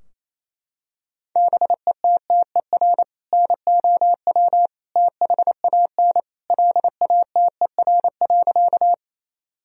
Morse code, heard as a keyed tone, about 28 wpm, 700 Hz. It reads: BETTER NOW THAN LATER.
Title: Sentences from Top 1000 Words - Review Part 4 - 28wpm